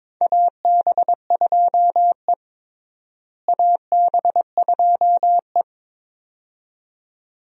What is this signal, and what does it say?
Morse code, heard as a keyed tone, about 22 wpm, 700 Hz. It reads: AB2E AB2E
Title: USA Call Signs - 4 Characters - Repeated 2x - 22wpm